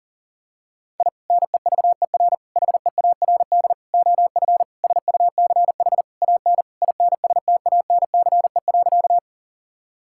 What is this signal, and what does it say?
Morse code, heard as a keyed tone, about 40 wpm, 700 Hz. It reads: I NEVER HEARD OF SUCH AN INSTANCE.
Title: Sentences from Top 1000 Words - Review Part 5 - 40wpm